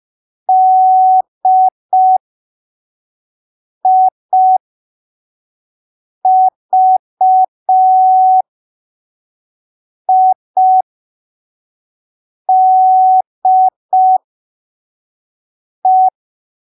Morse code, heard as a keyed tone, about 5 wpm, spaced as for 3 wpm, 750 Hz. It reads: DIVIDE
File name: arquivos / texto / texto 29 05ppm